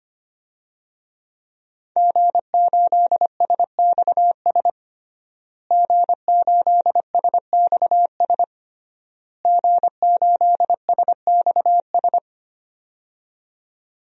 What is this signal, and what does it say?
Morse code, heard as a keyed tone, about 25 wpm, 700 Hz. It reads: G8SXS G8SXS G8SXS
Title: DXCC-100 Call Signs - 5 Characters - Repeated 3x - 25wpm